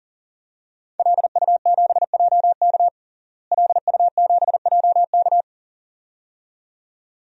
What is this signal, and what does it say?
Morse code, heard as a keyed tone, about 40 wpm, 700 Hz. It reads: LU7JK LU7JK
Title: DXCC-100 Call Signs - 5 Characters - Repeated 2x - 40wpm